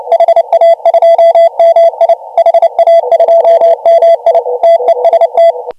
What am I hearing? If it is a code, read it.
Morse code, heard as a keyed tone, about 27 wpm, 670 Hz. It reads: HA2MI HA2MI TEST
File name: AUDIO2019 / HG1G CW